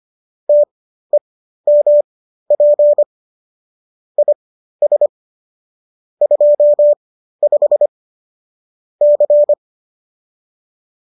Morse code, heard as a keyed tone, about 25 wpm, spaced as for 13 wpm, 590 Hz. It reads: TEMP IS 25 C